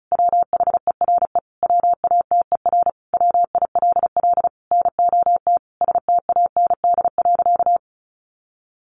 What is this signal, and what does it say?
Morse code, heard as a keyed tone, about 34 wpm, 700 Hz. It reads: WHERE WATER WILL NOT STAND.
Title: Sentences from Top 1300 Words 35wpm